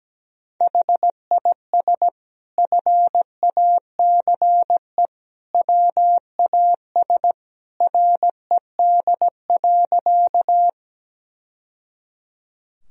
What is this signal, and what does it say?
Morse code, heard as a keyed tone, about 17 wpm, 700 Hz. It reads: HIS FACE WAS RED.
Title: Sentences from Top 1000 Words - Review Part 2 - 17wpm